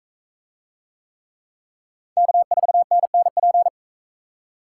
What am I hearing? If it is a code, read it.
Morse code, heard as a keyed tone, about 40 wpm, 700 Hz. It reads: K4NNP